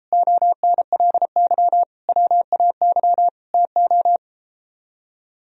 Morse code, heard as a keyed tone, about 33 wpm, 700 Hz. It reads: ONLY WAY TO